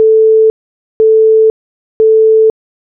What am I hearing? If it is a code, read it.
There is no code.